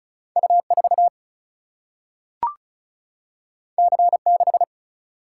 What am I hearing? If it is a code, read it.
Morse code, heard as a keyed tone, about 36 wpm, 700 Hz. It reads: U 4 C 6
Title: Single-Letter Single-Number 35wpm